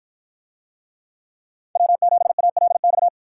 Morse code, heard as a keyed tone, about 50 wpm, 700 Hz. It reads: W7ALX